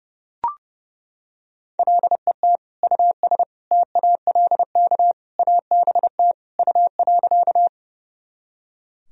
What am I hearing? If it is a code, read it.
Morse code, heard as a keyed tone, about 32 wpm, 700 Hz. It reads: LET US TALK ABT U.
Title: Sentences from Top 400 Words and Abbreviations 30wpm